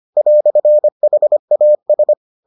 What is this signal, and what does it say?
Morse code, heard as a keyed tone, about 25 wpm, 600 Hz. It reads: "HAS